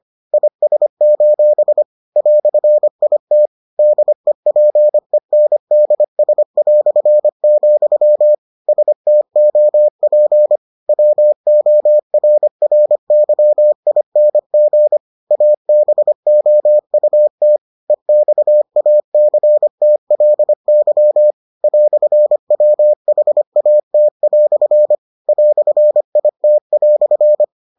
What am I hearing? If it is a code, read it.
Morse code, heard as a keyed tone, about 25 wpm, 600 Hz. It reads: IS: "IT DEPENDS", STOP WORRYING ABOUT EXACTLY "WHAT" "IT"